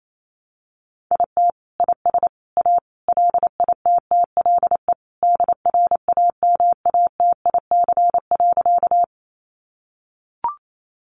Morse code, heard as a keyed tone, about 28 wpm, 700 Hz. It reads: IT IS A LITTLE DRAMATIC.
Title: Sentences from Top 1300 Words 28wpm